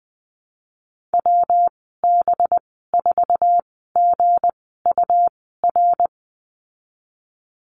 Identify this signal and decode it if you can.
Morse code, heard as a keyed tone, about 20 wpm, 700 Hz. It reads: W B 4 G U R
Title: 238 Lesson 38g - Letter q - Call Signs 20wpm